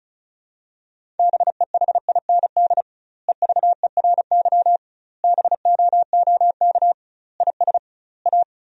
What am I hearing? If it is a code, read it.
Morse code, heard as a keyed tone, about 34 wpm, 700 Hz. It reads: BEHIND EVERY BOOK IS A
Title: 2x Word-Spacing - Sentences from Top 500 Words - 35wpm